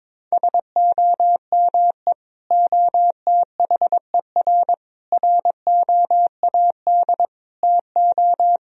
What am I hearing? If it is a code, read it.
Morse code, heard as a keyed tone, about 22 wpm, 700 Hz. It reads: SOME OTHER ROAD TO
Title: Sentences from Top 1000 Words - Review Part 4 - 22wpm